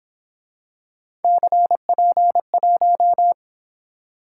Morse code, heard as a keyed tone, about 27 wpm, 700 Hz. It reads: CP1
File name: Speed-Racing - Three Letter-Number Combinations - 22wpm